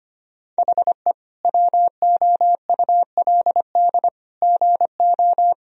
Morse code, heard as a keyed tone, about 25 wpm, 700 Hz. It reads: HE WOULD GO
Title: Sentences from Top 1000 Words - Review Part 4 - 25wpm